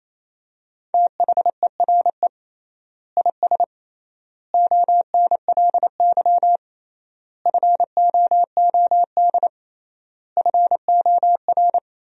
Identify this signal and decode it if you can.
Morse code, heard as a keyed tone, about 27 wpm, 700 Hz. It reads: THERE IS ONLY FOOD FOR
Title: 3x Word-Spacing - Sentences from Top 300 Words - 28wpm